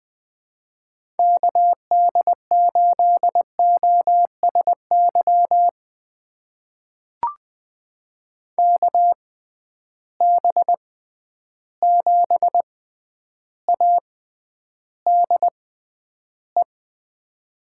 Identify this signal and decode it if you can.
Morse code, heard as a keyed tone, about 20 wpm, spaced as for 10 wpm, 700 Hz. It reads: KD8OSY KB7ADE
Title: USA Call Signs - 6 Characters - 6x Character Spacing - 20wpm